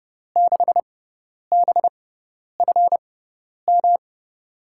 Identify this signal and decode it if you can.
Morse code, heard as a keyed tone, about 31 wpm, spaced as for 10 wpm, 700 Hz. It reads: 6BFM
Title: USA Call Signs - 6 Characters - 6x Character Spacing - 30wpm